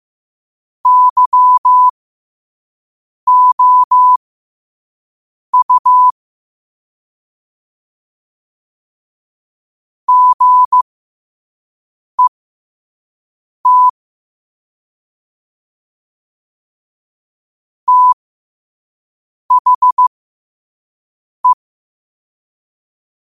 Morse code, heard as a keyed tone, about 15 wpm, spaced as for 5 wpm, 1000 Hz. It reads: YOU GET THE
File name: qotdfiles / 5wpm